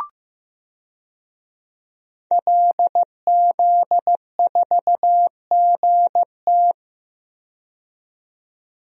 Morse code, heard as a keyed tone, about 15 wpm, 700 Hz. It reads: LZ4GT
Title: DXCC-100 Call Signs - 5 Characters - 15wpm